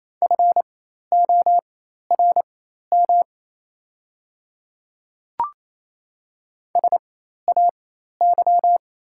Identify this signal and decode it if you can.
Morse code, heard as a keyed tone, about 28 wpm, spaced as for 13 wpm, 700 Hz. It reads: FORM SAY
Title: 4x Character Spacing - Top 200 Words - 28wpm